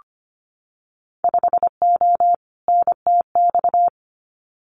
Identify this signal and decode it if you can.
Morse code, heard as a keyed tone, about 25 wpm, 690 Hz. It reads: 5O NTX
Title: Winter Field Day Exchanges Encore 25wpm